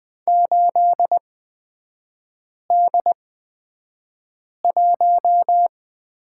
Morse code, heard as a keyed tone, about 20 wpm, spaced as for 5 wpm, 700 Hz. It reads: 8D1
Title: 224 Lesson 36e - BT - Three Characters 20wpm